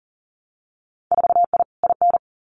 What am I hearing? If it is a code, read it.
Morse code, heard as a keyed tone, about 39 wpm, 700 Hz. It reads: 4I IN
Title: Winter Field Day Exchanges Encore 40wpm